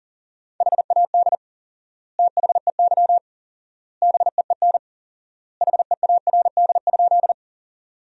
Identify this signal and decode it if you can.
Morse code, heard as a keyed tone, about 40 wpm, 700 Hz. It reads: HAD THEY BEEN HEARD?
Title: 4x Word-Spacing - Sentences from Top 400 Words - 40wpm